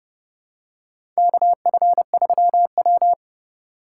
Morse code, heard as a keyed tone, about 29 wpm, 700 Hz. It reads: KF3W
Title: USA Call Signs - 4 Characters - 8x Character Spacing - 30wpm